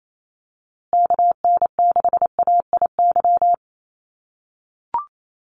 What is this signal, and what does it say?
Morse code, heard as a keyed tone, about 28 wpm, 700 Hz. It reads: KN6AIY
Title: USA Call Signs Repeated 2x 28wpm